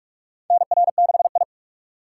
Morse code, heard as a keyed tone, about 43 wpm, 700 Hz. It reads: NR6I
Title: USA Call Signs - 4 Characters - 6x Character Spacing - 45wpm